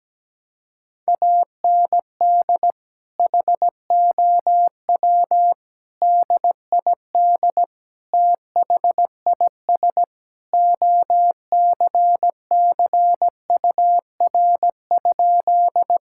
Morse code, heard as a keyed tone, about 17 wpm, 700 Hz. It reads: AND HOW DID THIS OCCUR?
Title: Sentences from Top 1000 Words - Review Part 6 - 17wpm